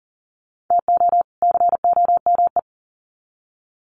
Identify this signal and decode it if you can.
Morse code, heard as a keyed tone, about 40 wpm, 700 Hz. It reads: TO COME
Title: Sets of 2 Words 40wpm